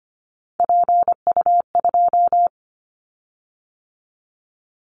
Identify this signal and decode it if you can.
Morse code, heard as a keyed tone, about 25 wpm, 700 Hz. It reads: PU2
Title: Callsign Prefixes for Top 50 DXCC - 25wpm